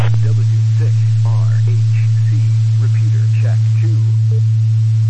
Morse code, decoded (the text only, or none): none